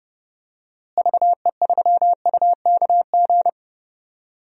Morse code, heard as a keyed tone, about 30 wpm, 700 Hz. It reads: VE3UKG
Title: DXCC-100 Call Signs - 6 Characters - Repeated 3x - 30wpm